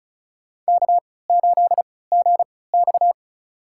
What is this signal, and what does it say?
Morse code, heard as a keyed tone, about 36 wpm, spaced as for 20 wpm, 700 Hz. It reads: K8GX
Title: USA Call Signs - 4 Characters - 3x Character Spacing - 35wpm